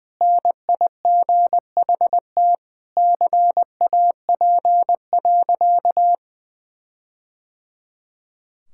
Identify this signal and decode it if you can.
Morse code, heard as a keyed tone, about 20 wpm, 700 Hz. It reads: NIGHT CAP.
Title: Sentences from Top 1600 Words 20wpm